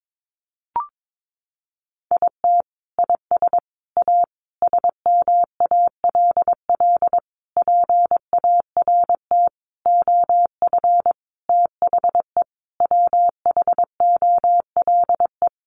Morse code, heard as a keyed tone, about 22 wpm, 700 Hz. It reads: IT IS A SMALL PART OF THE WHOLE
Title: Sentences from Top 1500 Words 22wpm